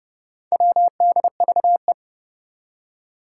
Morse code, heard as a keyed tone, about 30 wpm, 690 Hz. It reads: WDVE
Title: Four Challenging Letter-Number Combinations 30wpm